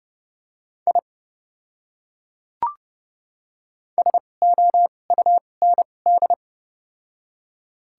Morse code, heard as a keyed tone, about 30 wpm, spaced as for 22 wpm, 700 Hz. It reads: I SOUND
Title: 2x Character Spacing - Top 100 Words - 30wpm